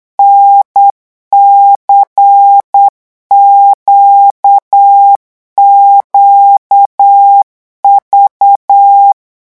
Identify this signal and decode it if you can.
Morse code, heard as a keyed tone, about 8 wpm, 800 Hz. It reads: NCQQV
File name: morse mp3 / revision3-vitesse 8 mots